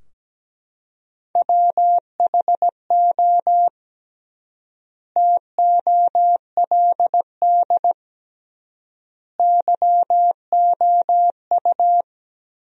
Morse code, heard as a keyed tone, about 17 wpm, 700 Hz. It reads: WHO TOLD YOU
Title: 3x Word-Spacing - Sentences from Top 400 Words - 17wpm